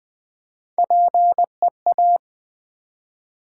Morse code, heard as a keyed tone, about 20 wpm, 700 Hz. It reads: PEA
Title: Build A Word - 3 Letters At A Time - 8-Letter Words - 20wpm